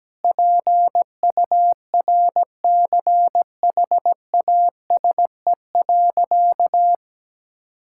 Morse code, heard as a keyed tone, about 17 wpm, 700 Hz. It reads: PURCHASE.
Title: Sentences from Top 1000 Words - Review Part 5 - 17wpm